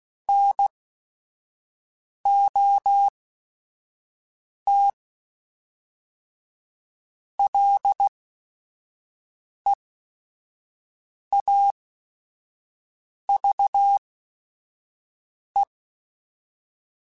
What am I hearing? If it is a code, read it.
Morse code, heard as a keyed tone, about 16 wpm, spaced as for 5 wpm, 800 Hz. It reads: NOT LEAVE
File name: Chuckles / Chuckles-mp3s / Temptation @5